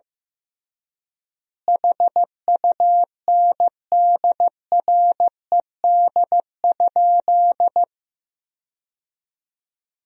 Morse code, heard as a keyed tone, about 15 wpm, 700 Hz. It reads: HUNDRED?
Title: 3x Word-Spacing - Sentences from Top 400 Words - 15wpm